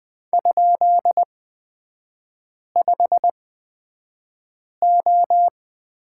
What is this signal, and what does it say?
Morse code, heard as a keyed tone, about 20 wpm, spaced as for 5 wpm, 700 Hz. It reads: ?5O
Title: 243 Lesson 39e - Letter z - Three Characters 20wpm